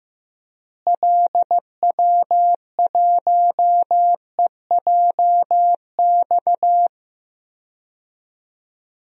Morse code, heard as a keyed tone, about 15 wpm, 700 Hz. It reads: LW1EJX
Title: DXCC-100 Call Signs - 6 Characters - Repeated 2x - 15wpm